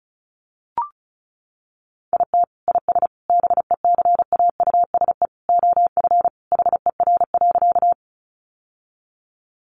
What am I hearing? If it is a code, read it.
Morse code, heard as a keyed tone, about 34 wpm, 700 Hz. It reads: IT IS BECAUSE OF HER.
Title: Sentences from Top 600 Words 35wpm